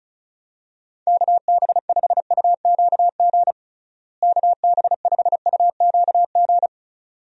Morse code, heard as a keyed tone, about 34 wpm, 700 Hz. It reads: KB5UQG KB5UQG
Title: USA Call Signs - 6 Characters - Repeated 2x - 35wpm